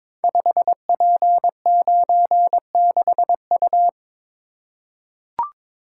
Morse code, heard as a keyed tone, about 23 wpm, 700 Hz. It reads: 5P96U
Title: Code Groups - 5 Letter:Numbers - 22wpm